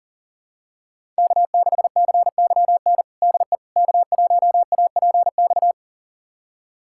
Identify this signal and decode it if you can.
Morse code, heard as a keyed tone, about 37 wpm, 700 Hz. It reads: K6CYN DE K1APX
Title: USA Callsign DE Callsign - 5 Characters - 40wpm